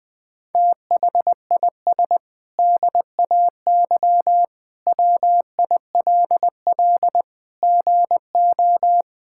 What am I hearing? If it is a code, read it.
Morse code, heard as a keyed tone, about 20 wpm, 700 Hz. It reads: THIS DAY WILL GO
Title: Sentences with 8-Word Phrases - 20wpm